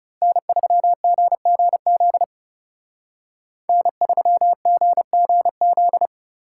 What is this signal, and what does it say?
Morse code, heard as a keyed tone, about 31 wpm, 700 Hz. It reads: N3GGZ N3GGZ
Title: Speed Racing - USA 1x3 Call Signs 30wpm